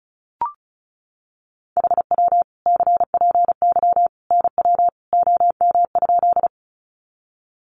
Morse code, heard as a keyed tone, about 34 wpm, 700 Hz. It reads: HW CPY NW OM?
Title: Common QSO Phrases 35wpm